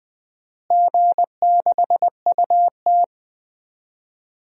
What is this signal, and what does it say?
Morse code, heard as a keyed tone, about 20 wpm, 700 Hz. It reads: G6UT
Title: DXCC-100 Call Signs - 4 Characters - Repeated 2x - 20wpm